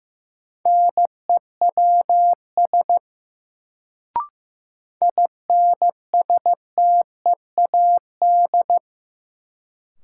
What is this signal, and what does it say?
Morse code, heard as a keyed tone, about 15 wpm, 700 Hz. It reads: NEWS INSTEAD
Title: Sentences from Top 800 Words - New Word Drill - 15wpm